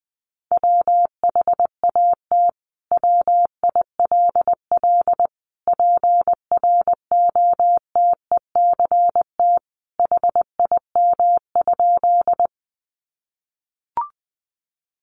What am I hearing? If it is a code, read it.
Morse code, heard as a keyed tone, about 20 wpm, 700 Hz. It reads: WHAT WILL PROTECT HIM?
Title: Sentences from Top 900 Words 20wpm